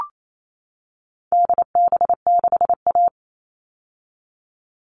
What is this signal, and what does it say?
Morse code, heard as a keyed tone, about 28 wpm, 700 Hz. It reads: DB6A
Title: All Cut Numbers - 4-Digit 28wpm